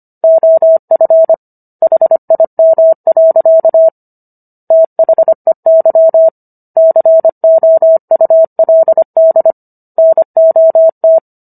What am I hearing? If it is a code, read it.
Morse code, heard as a keyed tone, about 25 wpm, 650 Hz. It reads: OF HIM. THEY COULD NOT
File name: Book One / 16-Chapter 16 - 25 WPM